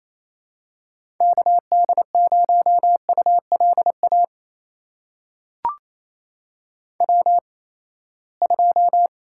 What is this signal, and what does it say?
Morse code, heard as a keyed tone, about 28 wpm, spaced as for 19 wpm, 700 Hz. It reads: KD0ULA W2
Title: USA Call Signs - 8x Character Spacing - 28wpm